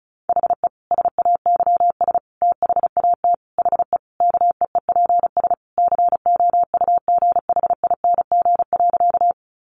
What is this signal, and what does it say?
Morse code, heard as a keyed tone, about 34 wpm, 700 Hz. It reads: HE SAYS THAT HE KEEPS COUGHING.
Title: Additional Easy Phrases 35wpm